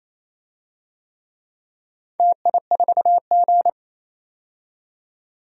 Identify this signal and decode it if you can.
Morse code, heard as a keyed tone, about 27 wpm, 700 Hz. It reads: TI4G
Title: DXCC-100 Call Signs - 4 Characters - 28wpm